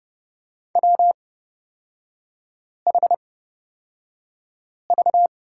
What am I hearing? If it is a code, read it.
Morse code, heard as a keyed tone, about 30 wpm, spaced as for 5 wpm, 700 Hz. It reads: WHV
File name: Lesson 36e - BT - Three Characters 30wpm